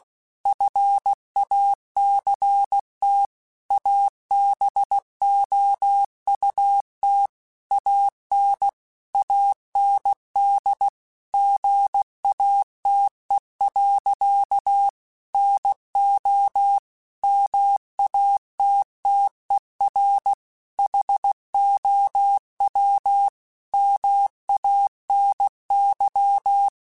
Morse code, heard as a keyed tone, about 16 wpm, 800 Hz. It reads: FACT ABOUT AN AND GATE. NO MATTER HOW MANY
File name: Technical Subjects / Technical Subjects-mp3s / AND Gate @15